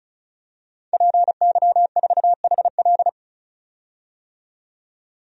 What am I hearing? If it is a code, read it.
Morse code, heard as a keyed tone, about 36 wpm, 700 Hz. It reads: PY4HL
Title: DXCC-100 Call Signs - 5 Characters - 35wpm